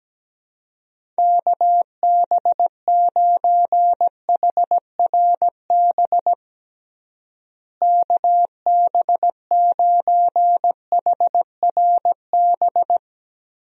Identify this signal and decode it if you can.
Morse code, heard as a keyed tone, about 17 wpm, 700 Hz. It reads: KB9HRB KB9HRB
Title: USA Call Signs - 6 Characters - Repeated 2x - 17wpm